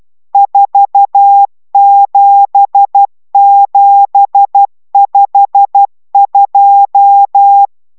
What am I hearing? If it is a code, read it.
Morse code, heard as a keyed tone, about 12 wpm, 800 Hz. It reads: 47752